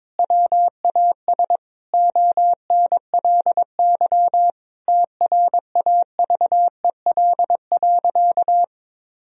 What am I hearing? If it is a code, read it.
Morse code, heard as a keyed tone, about 22 wpm, 700 Hz. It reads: WAS ONLY TRAVEL.
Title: Sentences from Top 1000 Words - Review Part 2 - 22wpm